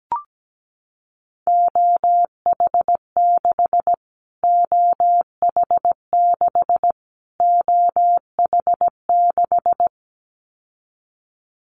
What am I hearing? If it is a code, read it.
Morse code, heard as a keyed tone, about 17 wpm, 700 Hz. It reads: OH6 OH6 OH6
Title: Callsign Prefixes for Top 25 DXCC - Repeated 3x - 17wpm